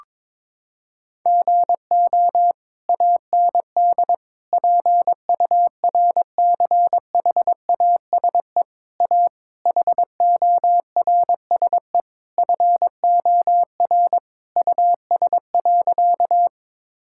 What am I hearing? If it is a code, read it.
Morse code, heard as a keyed tone, about 22 wpm, 700 Hz. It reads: GO AND PURCHASE A HORSE FOR US.
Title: Sentences from Top 1000 Words - Review Part 5 - 22wpm